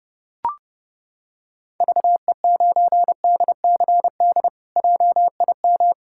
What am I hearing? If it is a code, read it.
Morse code, heard as a keyed tone, about 31 wpm, 700 Hz. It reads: VE9DCD JIM
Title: K1USN SST QSO Practice - 30wpm